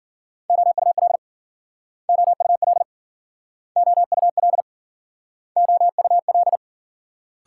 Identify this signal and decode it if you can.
Morse code, heard as a keyed tone, about 54 wpm, 700 Hz. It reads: OUL OUL OUL OUL